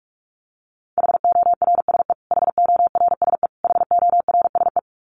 Morse code, heard as a keyed tone, about 44 wpm, 700 Hz. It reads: HORSE HORSE HORSE
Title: Top 500 Words Repeated 45wpm